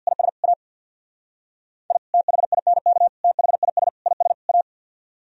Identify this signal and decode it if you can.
Morse code, heard as a keyed tone, about 63 wpm, 700 Hz. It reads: IS A I THINK THIS IS A